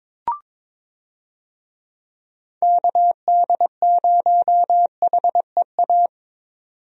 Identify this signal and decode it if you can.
Morse code, heard as a keyed tone, about 22 wpm, 700 Hz. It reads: KD0HEA
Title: USA Call Signs - 6 Characters - 22wpm